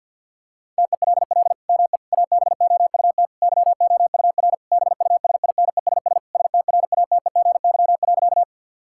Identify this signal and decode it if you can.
Morse code, heard as a keyed tone, about 50 wpm, 700 Hz. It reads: TELL ME ABOUT YOUR BUSINESS STRATEGY.
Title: Sentences from Top 1000 Words - Review Part 5 - 50wpm